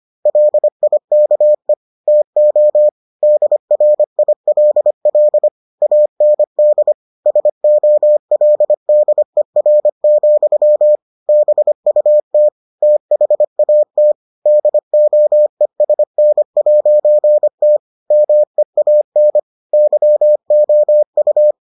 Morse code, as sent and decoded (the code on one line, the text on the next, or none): .-.. .. -.- . / - --- / -.. .-. .. .-.. .-.. / .- -. -.. / ... --- .-.. -.. .-. --..-- / -... ..- - / - .... .- - / -.. --- . ... -. .----. - / -- . .- -. / -.-- --- ..-
LIKE TO DRILL AND SOLDER, BUT THAT DOESN'T MEAN YOU